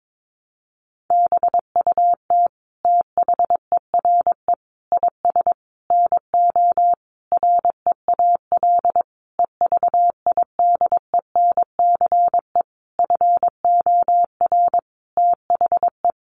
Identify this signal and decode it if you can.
Morse code, heard as a keyed tone, about 22 wpm, 700 Hz. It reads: BUT THERE IS NO REAL EVIDENCE FOR THE
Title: Sentences from Top 1300 Words 22wpm